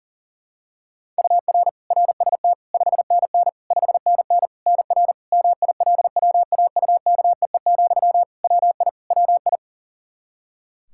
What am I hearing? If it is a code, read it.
Morse code, heard as a keyed tone, about 40 wpm, 700 Hz. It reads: UR RST 5NN 5NN NR MILWAUKEE, WI WI
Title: Practice QSO - Exchange 1 - Beginning - 40wpm